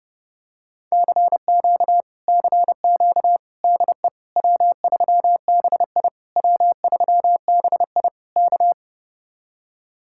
Morse code, heard as a keyed tone, about 32 wpm, 700 Hz. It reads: CQ CQ DE W3BI W3BI K